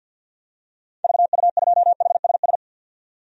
Morse code, heard as a keyed tone, about 51 wpm, 700 Hz. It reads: VU2HSS
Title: DXCC-100 Call Signs - 6 Characters - 50wpm